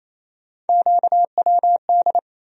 Morse code, heard as a keyed tone, about 28 wpm, 700 Hz. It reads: QWD
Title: USA Call Sign Postfixes - 3 Letters 28wpm